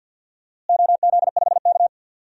Morse code, heard as a keyed tone, about 47 wpm, 700 Hz. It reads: OZ5K
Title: DXCC-100 Call Signs - 4 Characters - Repeated 3x - 50wpm